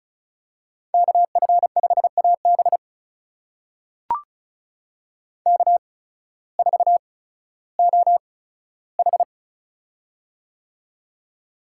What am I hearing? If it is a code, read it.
Morse code, heard as a keyed tone, about 36 wpm, spaced as for 16 wpm, 700 Hz. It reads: KF5AB K4OH
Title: USA Call Signs - 8x Character Spacing - 35wpm